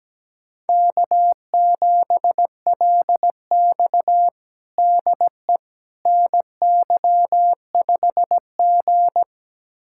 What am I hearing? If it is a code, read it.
Morse code, heard as a keyed tone, about 17 wpm, 700 Hz. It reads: K7LX DE NY5G
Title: USA Callsign DE Callsign - 4 Characters - 17wpm